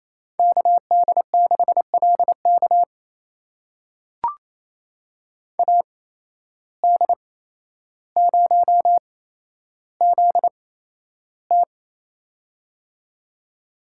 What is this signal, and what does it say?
Morse code, heard as a keyed tone, about 28 wpm, spaced as for 12 wpm, 700 Hz. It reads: KD6LK AD0ZT
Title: USA Call Signs - 8x Character Spacing - 28wpm